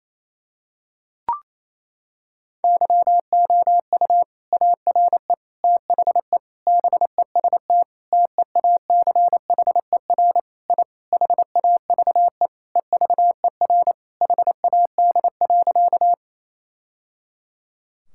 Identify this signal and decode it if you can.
Morse code, heard as a keyed tone, about 27 wpm, 700 Hz. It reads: YOU ARE THE BEST TEACHER I HAVE EVER HAD.
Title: Sentences from Top 1000 Words - Review Part 6 - 28wpm